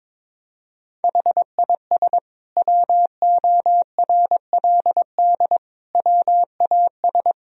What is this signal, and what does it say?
Morse code, heard as a keyed tone, about 22 wpm, 700 Hz. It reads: HIS WORLD WAS
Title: Sentences from Top 1000 Words - Review Part 1 - 22wpm